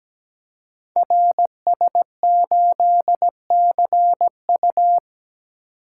Morse code, heard as a keyed tone, about 17 wpm, 700 Hz. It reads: RS8CU
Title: Code Groups - 5 Letter:Numbers - 17wpm